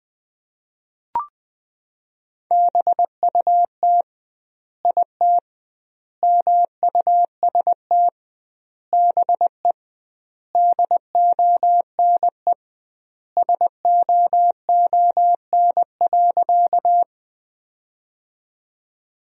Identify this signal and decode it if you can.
Morse code, heard as a keyed tone, about 20 wpm, 700 Hz. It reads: BUT IT MUST BE DONE SOON.
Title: 2x Word-Spacing - Sentences from Top 500 Words - 20wpm